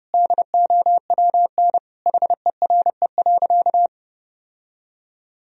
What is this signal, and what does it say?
Morse code, heard as a keyed tone, about 30 wpm, 700 Hz. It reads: DOWN HERE.
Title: Sentences from Top 1000 Words - Review Part 2 - 30wpm